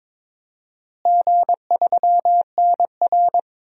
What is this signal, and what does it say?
Morse code, heard as a keyed tone, about 22 wpm, 700 Hz. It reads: G3NR
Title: DXCC-100 Call Signs - 4 Characters - Repeated 2x - 22wpm